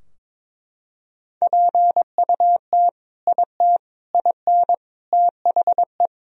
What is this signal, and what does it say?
Morse code, heard as a keyed tone, about 21 wpm, 700 Hz. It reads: PUT IT IN THE
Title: Sentences from Top 1000 Words - Review Part 2 - 22wpm